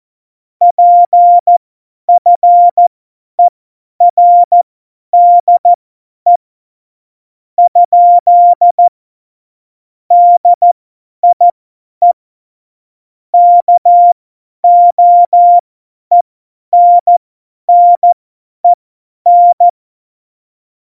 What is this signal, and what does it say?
Morse code, heard as a keyed tone, about 14 wpm, spaced as for 10 wpm, 700 Hz. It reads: PFERDE ? DIE KOENNEN
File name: Morsekurs / Witze / Witz 07. Mai 2019, 50 F70